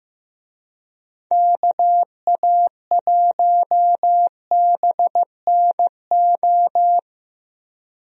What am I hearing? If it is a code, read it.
Morse code, heard as a keyed tone, about 15 wpm, 700 Hz. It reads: KA1BNO
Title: USA Call Signs - 6 Characters - Repeated 3x - 15wpm